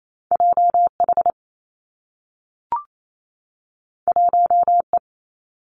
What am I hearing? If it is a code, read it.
Morse code, heard as a keyed tone, about 28 wpm, 700 Hz. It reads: J H 1 E
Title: Two Challenging Letter-Number Combinations 28wpm